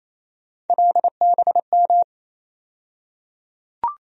Morse code, heard as a keyed tone, about 28 wpm, 700 Hz. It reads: LBM
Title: USA Call Sign Postfixes - 3 Letters 28wpm